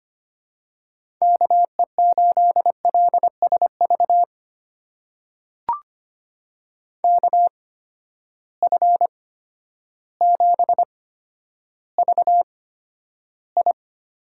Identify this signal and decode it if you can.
Morse code, heard as a keyed tone, about 25 wpm, spaced as for 11 wpm, 700 Hz. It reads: KE8LSV KF7VI
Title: USA Call Signs - 6 Characters - 8x Character Spacing - 25wpm